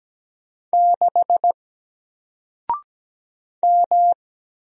Morse code, heard as a keyed tone, about 17 wpm, 700 Hz. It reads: 6 M
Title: Single Character (Letter:Number:Punctuation:Prosign) 17wpm